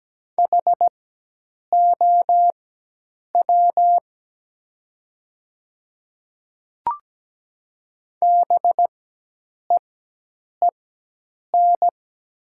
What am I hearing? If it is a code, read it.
Morse code, heard as a keyed tone, about 17 wpm, spaced as for 8 wpm, 700 Hz. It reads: HOW BEEN
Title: 4x Character Spacing - Top 100 Words - 17wpm